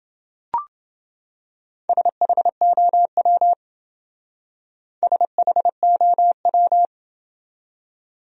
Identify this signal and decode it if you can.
Morse code, heard as a keyed tone, about 28 wpm, 700 Hz. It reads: SHOW SHOW